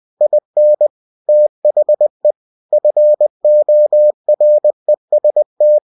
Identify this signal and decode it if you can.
Morse code, heard as a keyed tone, about 20 wpm, 600 Hz. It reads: IN THE FOREST
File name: cw-books / winnie-the-pooh / winnie-20wpm-page-0029